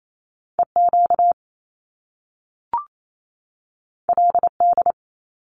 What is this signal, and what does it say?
Morse code, heard as a keyed tone, about 28 wpm, 700 Hz. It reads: EQ LD
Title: Top Two Letter Combinations in Words 28wpm